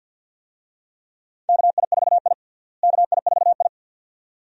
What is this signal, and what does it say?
Morse code, heard as a keyed tone, about 52 wpm, 700 Hz. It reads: KI4I KI4I
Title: USA Call Signs - 4 Characters - Repeated 2x - 50wpm